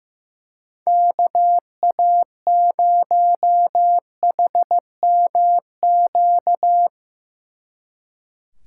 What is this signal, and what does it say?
Morse code, heard as a keyed tone, about 16 wpm, 700 Hz. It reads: KA0HMQ
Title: Speed Racing - USA 2x3 Call Signs 15wpm